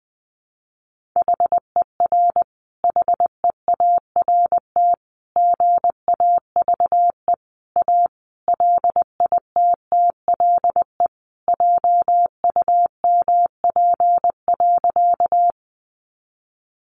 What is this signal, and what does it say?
Morse code, heard as a keyed tone, about 20 wpm, 700 Hz. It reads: HER HEART GAVE A LITTLE JUMP.
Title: Sentences from Top 1300 Words 20wpm